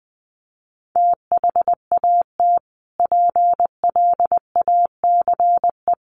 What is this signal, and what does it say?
Morse code, heard as a keyed tone, about 20 wpm, 700 Hz. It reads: THAT PLACE
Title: Sentences from Top 1500 Words 20wpm